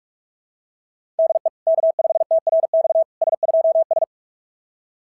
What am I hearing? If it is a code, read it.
Morse code, heard as a keyed tone, about 44 wpm, 690 Hz. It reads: DE K5TRX S2S
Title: SOTA QSO Practice 45wpm